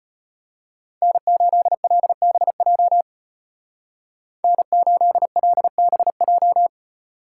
Morse code, heard as a keyed tone, about 33 wpm, 700 Hz. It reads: N8LBJ N8LBJ